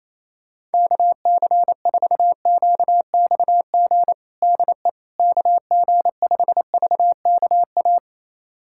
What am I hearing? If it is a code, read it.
Morse code, heard as a keyed tone, about 28 wpm, 700 Hz. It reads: KC4QXG DE KG5VKA
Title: USA Callsign DE Callsign - 6 Characters - 28wpm